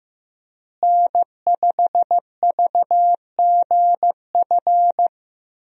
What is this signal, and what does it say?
Morse code, heard as a keyed tone, about 15 wpm, 700 Hz. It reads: N5VGF
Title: USA Call Signs - 5 Characters - Repeated 3x - 15wpm